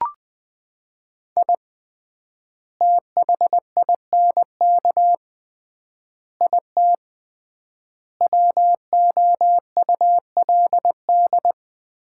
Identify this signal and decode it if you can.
Morse code, heard as a keyed tone, about 20 wpm, 700 Hz. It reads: I THINK IT WOULD
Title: 3x Word-Spacing - Sets of 5 Words - 20wpm